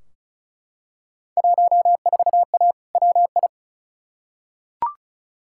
Morse code, heard as a keyed tone, about 35 wpm, 700 Hz. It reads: 14A WI
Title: Field Day Exchanges Encore - 35wpm